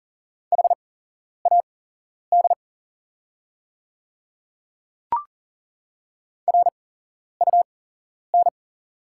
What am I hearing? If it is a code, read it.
Morse code, heard as a keyed tone, about 40 wpm, spaced as for 11 wpm, 690 Hz. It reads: HAD RUN
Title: 8x Character Spacing - Top 500 Words - 40wpm